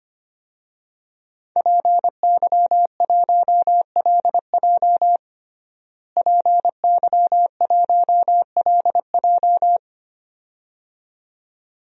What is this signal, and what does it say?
Morse code, heard as a keyed tone, about 25 wpm, 700 Hz. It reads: PY1LJ PY1LJ
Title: DXCC-100 Call Signs - 5 Characters - Repeated 2x - 25wpm